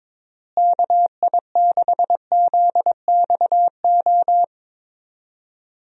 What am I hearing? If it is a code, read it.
Morse code, heard as a keyed tone, about 22 wpm, 700 Hz. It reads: KI6ZXO
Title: USA Call Signs - 6 Characters - Repeated 3x - 22wpm